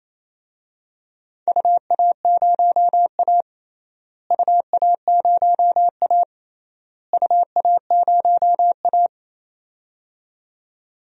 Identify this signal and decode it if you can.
Morse code, heard as a keyed tone, about 28 wpm, 700 Hz. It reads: UA0A UA0A UA0A
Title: DXCC-100 Call Signs - 4 Characters - Repeated 3x - 28wpm